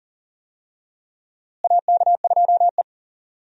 Morse code, heard as a keyed tone, about 40 wpm, 700 Hz. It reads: AK2E
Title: USA Call Signs - 4 Characters - Repeated 2x - 40wpm